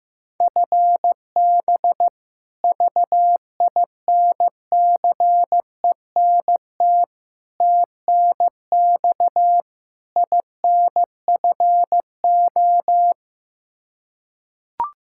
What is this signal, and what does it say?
Morse code, heard as a keyed tone, about 15 wpm, 700 Hz. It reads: FB VINCENT TNX INFO
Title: Practice QSO - Exchange 3 - More Info - 15wpm